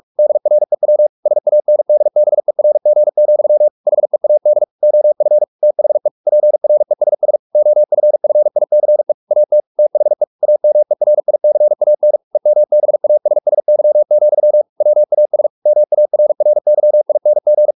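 Morse code, heard as a keyed tone, about 44 wpm, 600 Hz. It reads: DREW SANDBERG, HEAD OF THE PRESS OFFICE AT THE AMERICAN EMBASSY, WAS MARRYING
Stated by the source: Morse practice at 45 wpm, the novel Never by Ken Follett in English, from CW med Gnister.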